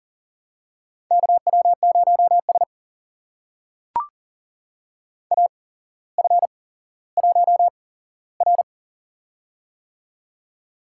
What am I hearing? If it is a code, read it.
Morse code, heard as a keyed tone, about 40 wpm, spaced as for 17 wpm, 700 Hz. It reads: KW0S AF1R
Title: USA Call Signs - 4 Characters - 8x Character Spacing - 40wpm